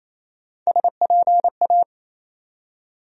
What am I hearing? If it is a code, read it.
Morse code, heard as a keyed tone, about 28 wpm, 700 Hz. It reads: SPA
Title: USA Call Sign Postfixes - 3 Letters 28wpm